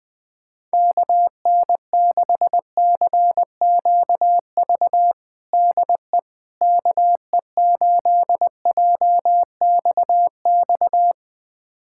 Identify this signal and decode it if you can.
Morse code, heard as a keyed tone, about 20 wpm, 700 Hz. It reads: KN6CQV DE KE8JXX
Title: USA Callsign DE Callsign - 6 Characters - 20wpm